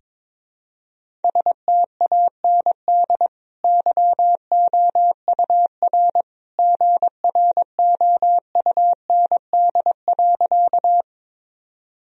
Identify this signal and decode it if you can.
Morse code, heard as a keyed tone, about 22 wpm, 700 Hz. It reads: STAND YOUR GROUND.
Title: Sentences from Top 1000 Words - Review Part 2 - 22wpm